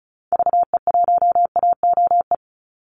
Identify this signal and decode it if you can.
Morse code, heard as a keyed tone, about 34 wpm, 700 Hz. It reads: VE1AOE
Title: Call Signs Repeated 3x - 35wpm